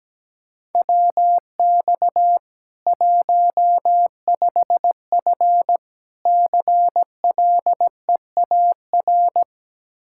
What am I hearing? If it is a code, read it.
Morse code, heard as a keyed tone, about 17 wpm, 700 Hz. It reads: WX 15F CLEAR
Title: Practice QSO - Exchange 2 - Info - 17wpm